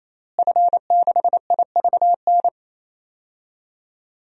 Morse code, heard as a keyed tone, about 28 wpm, 690 Hz. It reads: F6IVN
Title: DXCC-100 Call Signs - 5 Characters - Repeated 3x - 28wpm